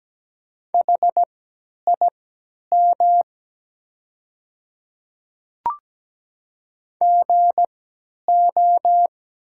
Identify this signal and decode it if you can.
Morse code, heard as a keyed tone, about 17 wpm, spaced as for 10 wpm, 690 Hz. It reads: HIM GO
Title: 3x Character Spacing - Top 500 Words - 17wpm